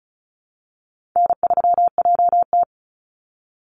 Morse code, heard as a keyed tone, about 34 wpm, 700 Hz. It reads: N3JT
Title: Call Signs Repeated 2x - 35wpm